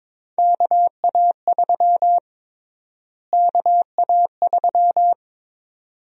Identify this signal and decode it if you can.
Morse code, heard as a keyed tone, about 22 wpm, 700 Hz. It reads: KA3 KA3
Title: USA Call Sign Prefixes - Repeated 2x - 22wpm